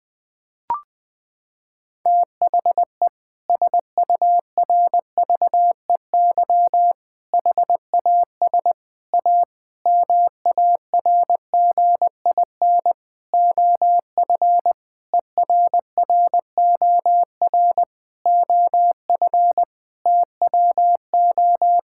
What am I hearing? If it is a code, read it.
Morse code, heard as a keyed tone, about 21 wpm, 700 Hz. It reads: THE SURVEY HAS A MARGIN OF ERROR OF TWO
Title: Sentences with 6-Word Phrases - 20wpm